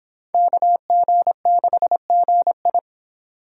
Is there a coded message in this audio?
Yes: Morse code from 0.3 to 2.8 s, heard as a keyed tone, about 29 wpm, 700 Hz. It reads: KG6GI